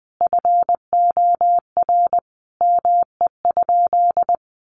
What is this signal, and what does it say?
Morse code, heard as a keyed tone, about 20 wpm, 700 Hz. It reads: FOR ME?
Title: Sentences from Top 800 Words 20wpm